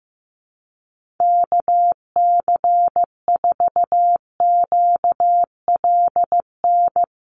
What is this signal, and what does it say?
Morse code, heard as a keyed tone, about 15 wpm, 700 Hz. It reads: KC4QLN
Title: USA Call Signs 15wpm